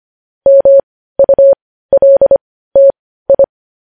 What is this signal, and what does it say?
Morse code, heard as a keyed tone, about 25 wpm, spaced as for 15 wpm, 550 Hz. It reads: MULTI
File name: podcast / News headlines-1x-25x15